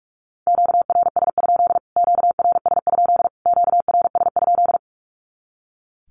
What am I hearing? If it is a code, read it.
Morse code, heard as a keyed tone, about 44 wpm, 700 Hz. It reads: QRS? QRS? QRS?
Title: Q-Codes Repeated 3x 45wpm